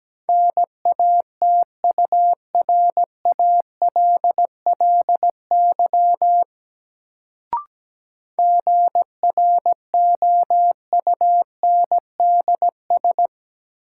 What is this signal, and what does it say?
Morse code, heard as a keyed tone, about 17 wpm, 700 Hz. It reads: NATURALLY GROUNDS
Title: Sentences from Top 1300 Words - New Word Drill Alt - 17wpm